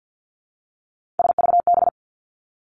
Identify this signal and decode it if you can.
Morse code, heard as a keyed tone, about 50 wpm, 700 Hz. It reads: SVB
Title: Three Challenging Letter-Number Combinations 50wpm